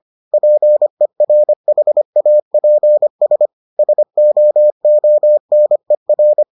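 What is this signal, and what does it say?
Morse code, heard as a keyed tone, about 25 wpm, 600 Hz. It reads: PERHAPS SOONER
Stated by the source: Morse edition of the Foundations of Amateur Radio episodes.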